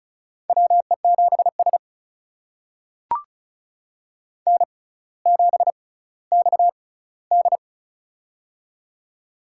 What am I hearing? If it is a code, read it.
Morse code, heard as a keyed tone, about 36 wpm, spaced as for 18 wpm, 700 Hz. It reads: WE7S N7XD
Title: USA Call Signs - 4 Characters - 6x Character Spacing - 35wpm